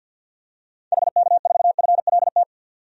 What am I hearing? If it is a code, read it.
Morse code, heard as a keyed tone, about 48 wpm, 700 Hz. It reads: HK4FLT
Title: DXCC-100 Call Signs - 6 Characters - 50wpm